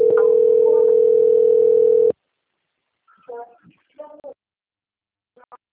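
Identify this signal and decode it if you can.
Ringback tone (North America), 440 + 480 Hz.
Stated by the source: Outgoing phone call recorded on a system labelled Telemarket Centro - RJ.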